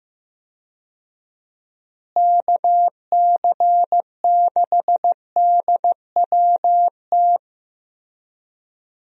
Morse code, heard as a keyed tone, about 15 wpm, 700 Hz. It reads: KC6DWT
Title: USA Call Signs - 6 Characters - Repeated 2x - 15wpm